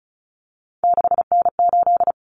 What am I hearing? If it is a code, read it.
Morse code, heard as a keyed tone, about 34 wpm, 700 Hz. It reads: 6N8
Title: Common Cut Numbers - 3-Digit 35wpm